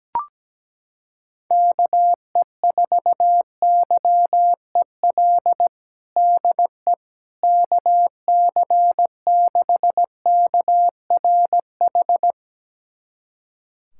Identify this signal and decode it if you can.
Morse code, heard as a keyed tone, about 17 wpm, 700 Hz. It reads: KE4YEL DE KC6KRH
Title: USA Callsign DE Callsign 17wpm